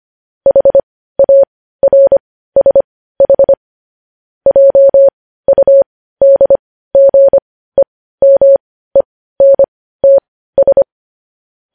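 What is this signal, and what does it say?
Morse code, heard as a keyed tone, about 25 wpm, spaced as for 15 wpm, 550 Hz. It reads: HARSH JUDGEMENTS
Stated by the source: News headlines in Morse code at 15 words per minute, characters sent at 25 WPM.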